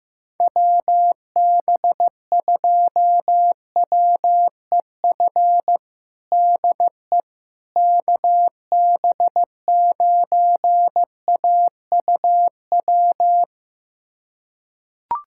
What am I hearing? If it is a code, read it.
Morse code, heard as a keyed tone, about 16 wpm, 700 Hz. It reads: WB2WEF DE KB9AUW
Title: USA Callsign DE Callsign 15wpm